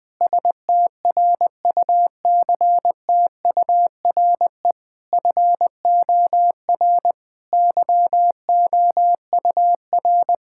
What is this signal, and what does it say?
Morse code, heard as a keyed tone, about 20 wpm, 700 Hz. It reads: STRUCTURE FOR YOUR
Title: Sentences from Top 1000 Words - Review Part 6 - 20wpm